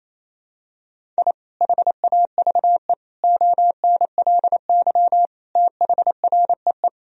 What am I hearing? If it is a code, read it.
Morse code, heard as a keyed tone, about 28 wpm, 700 Hz. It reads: I HAVE ONLY THREE
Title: Sentences from Top 1000 Words - Review Part 1 - 28wpm